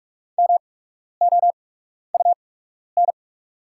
Morse code, heard as a keyed tone, about 44 wpm, spaced as for 12 wpm, 690 Hz. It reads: MOUN
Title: 8x Character Spacing - Top 300 Words - 45wpm